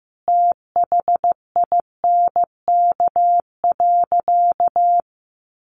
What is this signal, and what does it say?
Morse code, heard as a keyed tone, about 15 wpm, 700 Hz. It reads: THINK.
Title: Sentences from Top 600 Words 15wpm